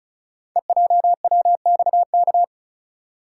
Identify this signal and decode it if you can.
Morse code, heard as a keyed tone, about 36 wpm, 690 Hz. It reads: EJWXK
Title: Code Groups - 5 Letters - 35wpm